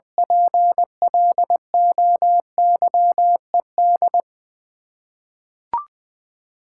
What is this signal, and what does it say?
Morse code, heard as a keyed tone, about 21 wpm, 700 Hz. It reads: PLOYED
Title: Ten Letter Words - 20wpm